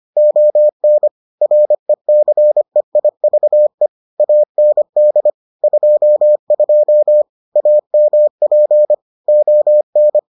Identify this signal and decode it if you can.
Morse code, heard as a keyed tone, about 25 wpm, 600 Hz. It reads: ON RECEIVE AND 22 AMP ON